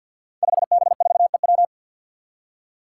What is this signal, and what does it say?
Morse code, heard as a keyed tone, about 52 wpm, 700 Hz. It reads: 5B4EW
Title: Five Challenging Letter-Number Combinations 50wpm